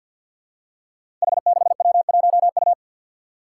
Morse code, heard as a keyed tone, about 53 wpm, 700 Hz. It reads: H6W1U
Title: Five Challenging Letter-Number Combinations 50wpm